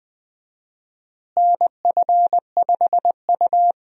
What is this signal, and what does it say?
Morse code, heard as a keyed tone, about 20 wpm, 700 Hz. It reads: NF5U
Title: USA Call Signs - 4 Characters - Repeated 3x - 20wpm